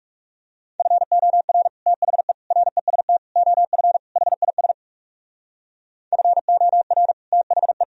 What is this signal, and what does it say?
Morse code, heard as a keyed tone, about 40 wpm, 700 Hz. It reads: FOR THE REST OF HIS FOR THE